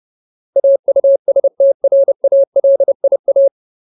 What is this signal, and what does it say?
Morse code, heard as a keyed tone, about 30 wpm, 550 Hz. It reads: AUSTRALIA
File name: Ultimate Word List / 30WPM Ultimate